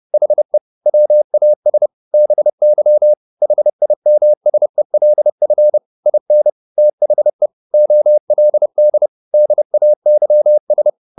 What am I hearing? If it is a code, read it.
Morse code, heard as a keyed tone, about 30 wpm, 600 Hz. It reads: HE WAS BY HIMSELF IN THE OLD DAYS
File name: the old man and the sea / the old man and the sea 1 30wpm